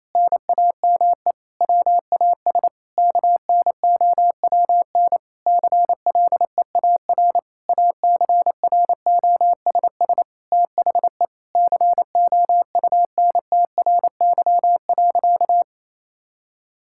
Morse code, heard as a keyed tone, about 28 wpm, 700 Hz. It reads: NAME WAS KNOWN CLEAR ACROSS THE COUNTRY.
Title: Sentences from Top 1000 Words - Review Part 5 - 28wpm